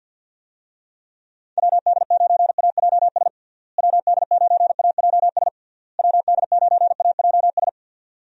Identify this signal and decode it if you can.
Morse code, heard as a keyed tone, about 48 wpm, 700 Hz. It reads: WD9AJS WD9AJS WD9AJS
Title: USA Call Signs - 6 Characters - Repeated 3x - 50wpm